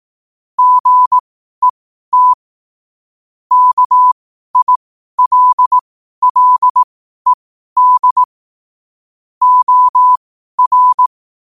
Morse code, heard as a keyed tone, about 18 wpm, spaced as for 12 wpm, 1000 Hz. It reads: GET KILLED OR
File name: qotdfiles / 12wpm